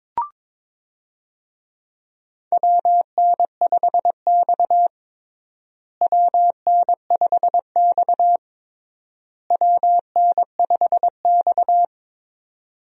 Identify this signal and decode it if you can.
Morse code, heard as a keyed tone, about 22 wpm, 700 Hz. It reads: WN5X WN5X WN5X
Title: USA Call Signs - 4 Characters - Repeated 3x - 22wpm